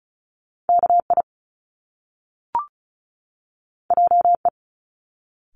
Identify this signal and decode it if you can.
Morse code, heard as a keyed tone, about 36 wpm, 700 Hz. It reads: KI JE
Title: Top Two Letter Combinations in Words 35wpm